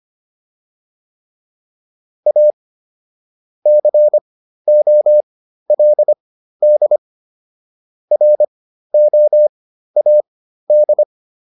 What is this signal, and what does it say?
Morse code, heard as a keyed tone, about 25 wpm, spaced as for 13 wpm, 600 Hz. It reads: A COLD ROAD